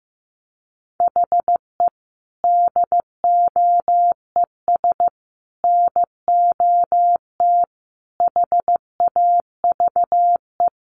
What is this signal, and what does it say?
Morse code, heard as a keyed tone, about 15 wpm, 700 Hz. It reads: HE DOES NOT HAVE
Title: Sets of 5 Words Encore 15wpm